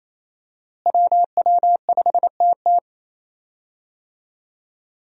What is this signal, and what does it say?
Morse code, heard as a keyed tone, about 28 wpm, 700 Hz. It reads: WW5TT
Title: USA Call Signs - 5 Characters - Repeated 3x - 28wpm